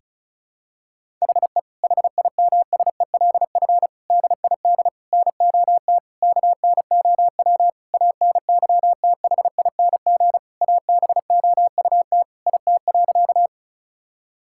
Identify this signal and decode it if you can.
Morse code, heard as a keyed tone, about 36 wpm, 700 Hz. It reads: HE HIMSELF DID NOT KNOW ANYTHING ABOUT IT.
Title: Sentences from Top 1600 Words 35wpm